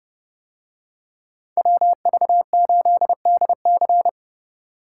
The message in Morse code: WV8DC

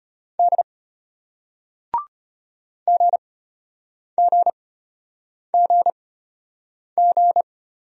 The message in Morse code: D GGGG

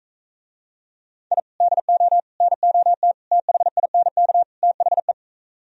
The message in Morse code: I DO NOT THINK THE

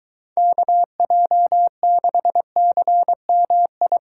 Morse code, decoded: KJ6CMI

